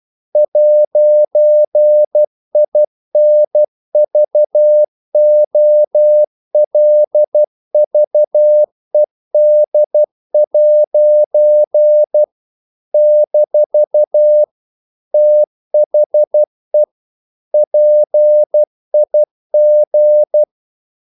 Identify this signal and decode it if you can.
Morse code, heard as a keyed tone, about 12 wpm, 600 Hz. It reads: 'INVOLVED' - THE PIG
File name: CW / Practice / EnglishQuotations0112